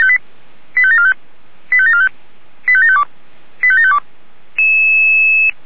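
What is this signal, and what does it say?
Five-tone selective call (ZVEI), heard as quick runs of five tones, tones of about 70 ms. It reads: 85745 85745 85752 85752
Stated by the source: ZVEI five-tone selective call received on an SDR.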